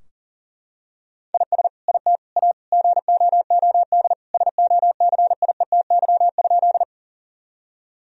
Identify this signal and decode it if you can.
Morse code, heard as a keyed tone, about 40 wpm, 700 Hz. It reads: IS IT A GOOD SOCIETY?
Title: Sentences from Top 1000 Words - Review Part 4 - 40wpm